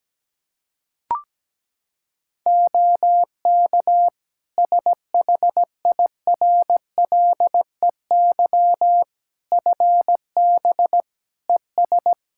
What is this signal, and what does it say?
Morse code, heard as a keyed tone, about 17 wpm, 700 Hz. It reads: OK SHIRLEY FB ES
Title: Practice QSO - Exchange 2 - Info - 17wpm